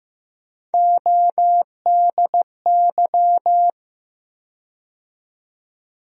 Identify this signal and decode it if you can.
Morse code, heard as a keyed tone, about 15 wpm, 700 Hz. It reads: ODY
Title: USA Call Sign Postfixes - 3 Letters 15wpm